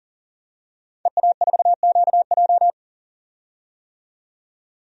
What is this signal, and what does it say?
Morse code, heard as a keyed tone, about 40 wpm, 700 Hz. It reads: EA4QJ
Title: DXCC-100 Call Signs - 5 Characters - 40wpm